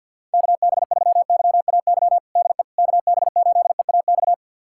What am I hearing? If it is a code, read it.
Morse code, heard as a keyed tone, about 47 wpm, 700 Hz. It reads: KB3YAY DE KB8EAX
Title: USA Callsign DE Callsign - 6 Characters - 50wpm